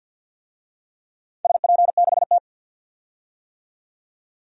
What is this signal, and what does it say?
Morse code, heard as a keyed tone, about 52 wpm, 700 Hz. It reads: SP6T
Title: DXCC-100 Call Signs - 4 Characters - 50wpm